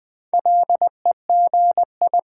Morse code, heard as a keyed tone, about 20 wpm, 700 Hz. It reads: LEGI
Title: Build A Word - 4 Letters At A Time - 11-Letter Words - 20wpm